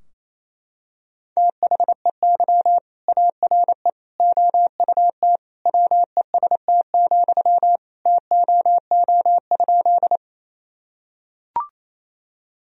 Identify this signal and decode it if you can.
Morse code, heard as a keyed tone, about 28 wpm, 700 Hz. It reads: THEY ARE OUT WEST, TOO?
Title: Sentences from Top 1000 Words - Review Part 2 - 28wpm